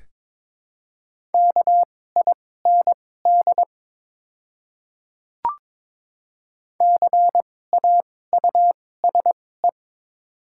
Morse code, heard as a keyed tone, about 21 wpm, spaced as for 16 wpm, 690 Hz. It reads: KIND CAUSE
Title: 2x Character Spacing - Top 500 Words - 22wpm